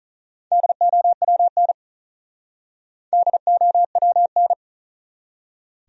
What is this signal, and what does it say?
Morse code, heard as a keyed tone, about 39 wpm, 700 Hz. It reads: DOWN DOWN
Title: Speed-Racing Top 200 Words 35wpm